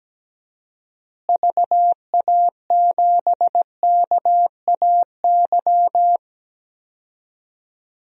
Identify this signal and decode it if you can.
Morse code, heard as a keyed tone, about 17 wpm, 700 Hz. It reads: VA7KAY
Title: DXCC-100 Call Signs - 6 Characters - Repeated 2x - 17wpm